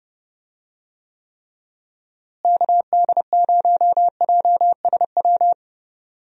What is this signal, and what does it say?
Morse code, heard as a keyed tone, about 30 wpm, 700 Hz. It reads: KD0JSW
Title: USA Call Signs - 6 Characters - Repeated 3x - 30wpm